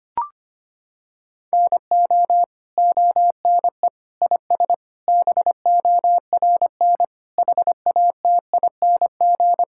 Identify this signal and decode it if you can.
Morse code, heard as a keyed tone, about 25 wpm, 700 Hz. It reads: NO ONE IS BORN HATING
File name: Sets of 9 Words - 25wpm